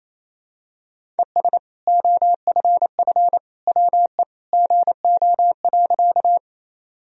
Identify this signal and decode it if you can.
Morse code, heard as a keyed tone, about 27 wpm, 700 Hz. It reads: ES OFF WE GO.
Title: Sentences from Top 200 Words and Abbreviations 28wpm